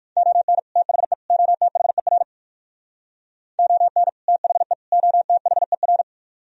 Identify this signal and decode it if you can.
Morse code, heard as a keyed tone, about 58 wpm, 700 Hz. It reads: ON THE OTHER ON THE OTHER